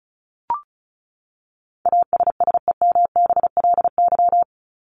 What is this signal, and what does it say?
Morse code, heard as a keyed tone, about 34 wpm, 700 Hz. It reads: ASSEMBLY
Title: LY Suffix - Top 100 Words with Base Words 35wpm